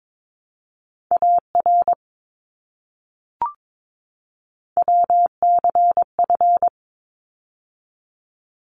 Morse code, heard as a keyed tone, about 22 wpm, 700 Hz. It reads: AR WCF
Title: ARRL Sections 22wpm